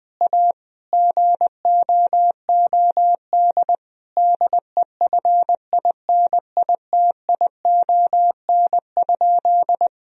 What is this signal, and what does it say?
Morse code, heard as a keyed tone, about 20 wpm, 700 Hz. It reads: A GOOD DEFINITION?
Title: Sentences from Top 1000 Words - Review Part 4 - 20wpm